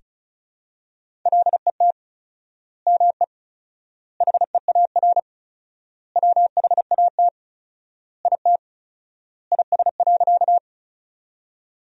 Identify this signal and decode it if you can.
Morse code, heard as a keyed tone, about 34 wpm, 700 Hz. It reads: LET ME HEAR WHAT IT IS.
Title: 4x Word-Spacing - Sentences from Top 300 Words - 35wpm